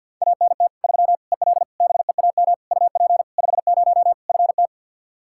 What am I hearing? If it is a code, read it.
Morse code, heard as a keyed tone, about 52 wpm, 700 Hz. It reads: ANT 3 EL BEAM UP 50 FT
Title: Practice QSO - Exchange 2 - Info - 50wpm